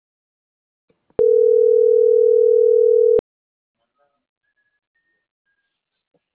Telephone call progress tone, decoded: ringback tone

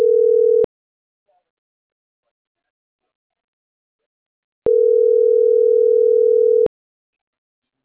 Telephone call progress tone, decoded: ringback tone

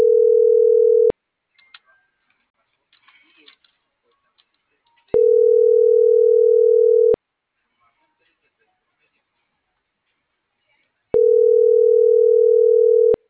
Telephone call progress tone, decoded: ringback tone

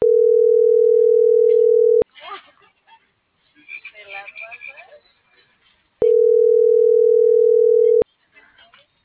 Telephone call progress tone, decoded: ringback tone

